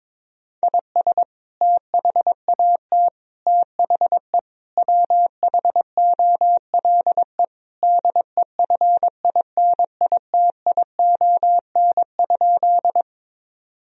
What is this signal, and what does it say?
Morse code, heard as a keyed tone, about 22 wpm, 700 Hz. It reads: IS THAT THE WHOLE DEFINITION?
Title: Sentences from Top 1000 Words - Review Part 4 - 22wpm